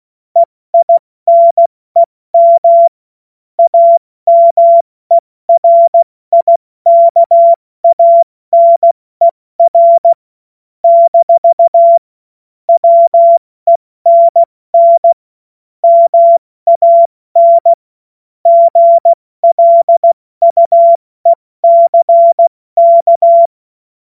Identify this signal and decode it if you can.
Morse code, heard as a keyed tone, about 16 wpm, 680 Hz. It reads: EINEM AMERIKANER - WENN MAN GLUECK
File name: Witz-05-August-2025-70-F80